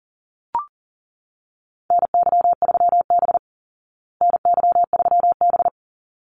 Morse code, heard as a keyed tone, about 40 wpm, 700 Hz. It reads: NY3B NY3B
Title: Call Signs Repeated 3x - 40wpm